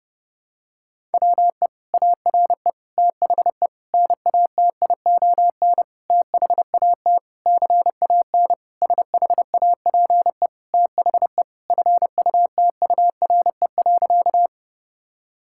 Morse code, heard as a keyed tone, about 30 wpm, 700 Hz. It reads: WE ARE THE NATION THAT CAN SHAPE THE FUTURE.
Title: Sentences from Top 1000 Words - Review Part 6 - 30wpm